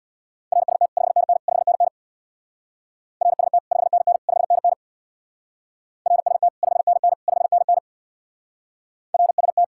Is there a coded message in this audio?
Yes: Morse code from 0.5 to 9.6 s, heard as a keyed tone, about 67 wpm, 700 Hz. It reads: RST 5NN 5NN RST 5NN 5NN RST 5NN 5NN RST